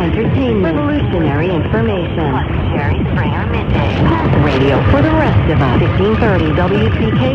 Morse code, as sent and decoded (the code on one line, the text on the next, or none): none